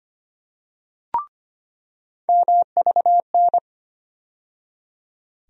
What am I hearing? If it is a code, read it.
Morse code, heard as a keyed tone, about 25 wpm, 700 Hz. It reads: MVN
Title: USA Call Sign Postfixes - 3 Letters 25wpm